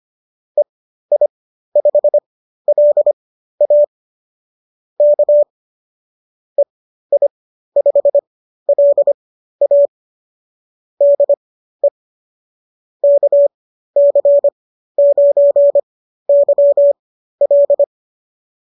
Morse code, heard as a keyed tone, about 25 wpm, spaced as for 13 wpm, 590 Hz. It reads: EI5LA K EI5LA DE KC9YL